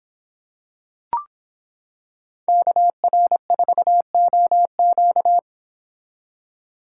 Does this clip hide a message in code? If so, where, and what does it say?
Yes: Morse code from 2.5 to 5.4 s, heard as a keyed tone, about 26 wpm, 700 Hz. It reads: KR4OQ